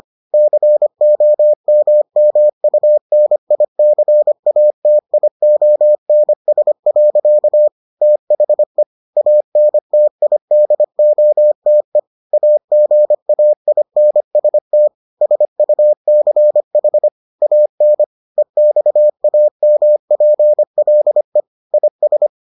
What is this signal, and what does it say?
Morse code, heard as a keyed tone, about 25 wpm, 600 Hz. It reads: COMMUNICATIONS. THE ANTIDOTE AGAINST SUCH AN EXAMPLE IS